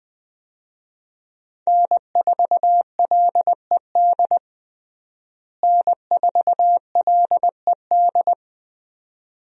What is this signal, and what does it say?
Morse code, heard as a keyed tone, about 21 wpm, 700 Hz. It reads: N4LED N4LED
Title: USA Call Signs - 5 Characters - Repeated 2x - 20wpm